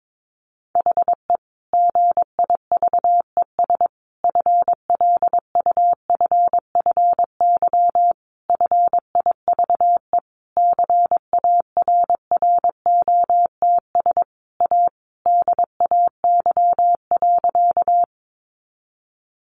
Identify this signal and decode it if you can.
Morse code, heard as a keyed tone, about 22 wpm, 700 Hz. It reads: HE GIVES FLUFFY FIVE CARROTS A DAY.
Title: Easy Phrases 22wpm